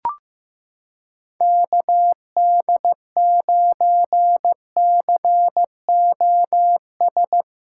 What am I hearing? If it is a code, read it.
Morse code, heard as a keyed tone, about 15 wpm, 700 Hz. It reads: KD9COS
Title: USA Callsign DE Callsign - 6 Characters - 15wpm